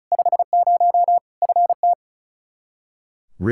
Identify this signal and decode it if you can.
Morse code, heard as a keyed tone, about 34 wpm, 700 Hz. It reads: 50 FT
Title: Practice QSO - Exchange 2 - Info - 35wpm